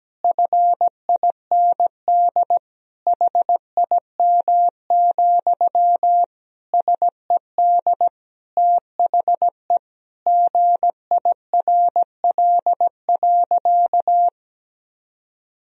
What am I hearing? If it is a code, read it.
Morse code, heard as a keyed tone, about 17 wpm, 700 Hz. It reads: FIND HIM, SED THE GIRL.
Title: Sentences from Top 400 Words and Abbreviations 17wpm